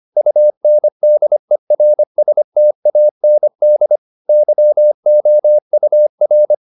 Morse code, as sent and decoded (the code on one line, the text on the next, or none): ..- -. -.. .-. ... - .- -. -.. / -.-- --- ..- .-.
UNDERSTAND YOUR